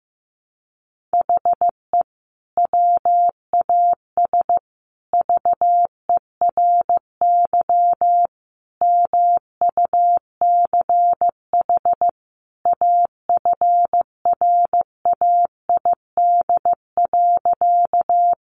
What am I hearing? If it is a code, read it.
Morse code, heard as a keyed tone, about 15 wpm, 700 Hz. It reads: HE WAS VERY MUCH AFRAID.
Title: Sentences from Top 1500 Words 15wpm